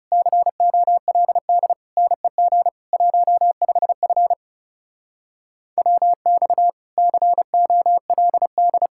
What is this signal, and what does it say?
Morse code, heard as a keyed tone, about 39 wpm, 700 Hz. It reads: COLD NEG 15F WX COLD